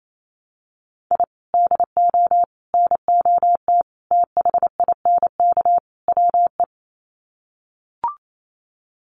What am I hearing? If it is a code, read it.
Morse code, heard as a keyed tone, about 28 wpm, 700 Hz. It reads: I DO NOT THINK WE
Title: Sets of 5 Words 28wpm